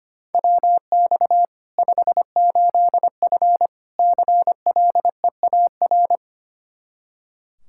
Morse code, heard as a keyed tone, about 25 wpm, 700 Hz. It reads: WX 58F CLEAR
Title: Practice QSO - Exchange 2 - Info - 25wpm